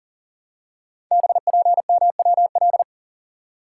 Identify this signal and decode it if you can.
Morse code, heard as a keyed tone, about 38 wpm, 690 Hz. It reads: BPMWL
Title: Code Groups - 5 Letters - 40wpm